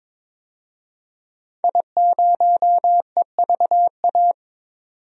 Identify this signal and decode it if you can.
Morse code, heard as a keyed tone, about 21 wpm, 700 Hz. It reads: I0EVA
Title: DXCC-100 Call Signs - 5 Characters - 22wpm